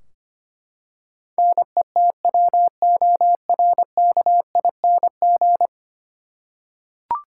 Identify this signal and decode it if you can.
Morse code, heard as a keyed tone, about 25 wpm, 700 Hz. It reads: NETWORKING
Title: Practice QSO - Exchange 3 - More Info - 25wpm